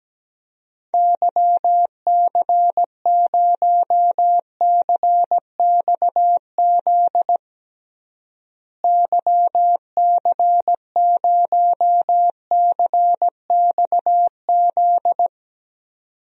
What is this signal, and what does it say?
Morse code, heard as a keyed tone, about 17 wpm, 700 Hz. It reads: YC0CXZ YC0CXZ
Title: DXCC-100 Call Signs - 6 Characters - Repeated 2x - 17wpm